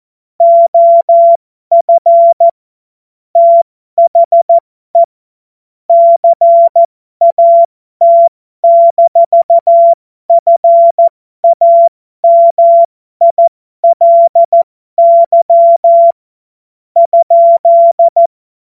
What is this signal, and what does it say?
Morse code, heard as a keyed tone, about 14 wpm, spaced as for 12 wpm, 680 Hz. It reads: OF THE CAT-FAMILY ?
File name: Morsekurs / Witze